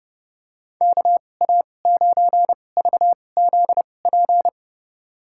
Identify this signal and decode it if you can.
Morse code, heard as a keyed tone, about 32 wpm, spaced as for 22 wpm, 700 Hz. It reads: KA9VZP
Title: USA Call Signs - 6 Characters - 2x Character Spacing - 30wpm